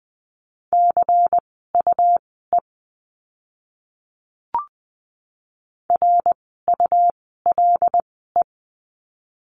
Morse code, heard as a keyed tone, about 20 wpm, spaced as for 15 wpm, 700 Hz. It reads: CUE RULE
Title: 101 Lesson 17f - Letter u - Words 20wpm